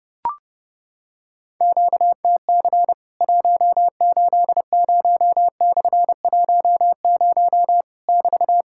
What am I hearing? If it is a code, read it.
Morse code, heard as a keyed tone, about 30 wpm, 700 Hz. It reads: QTC 180/10 <BT>
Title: Worked All Europe DX Contest - QTC Exchange - 30wpm